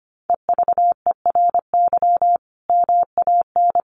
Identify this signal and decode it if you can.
Morse code, heard as a keyed tone, about 25 wpm, 700 Hz. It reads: EVERY MAN
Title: Sentences from Top 1400 Words 25wpm